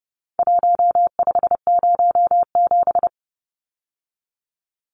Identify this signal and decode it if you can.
Morse code, heard as a keyed tone, about 30 wpm, 700 Hz. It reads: 1507
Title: Numbers - Contest Style - 4-Digit 30wpm